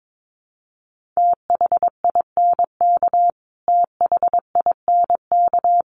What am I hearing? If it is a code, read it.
Morse code, heard as a keyed tone, about 22 wpm, 700 Hz. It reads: THINK THINK
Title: Top 500 Words Repeated 22wpm